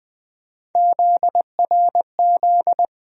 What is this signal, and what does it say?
Morse code, heard as a keyed tone, about 21 wpm, 700 Hz. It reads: ZRZ
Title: USA Call Sign Postfixes - 3 Letters 20wpm